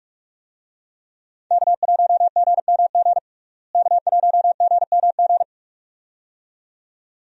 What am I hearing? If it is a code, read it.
Morse code, heard as a keyed tone, about 47 wpm, 700 Hz. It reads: K1GMG K1GMG
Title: USA Call Signs - 5 Characters - Repeated 2x - 45wpm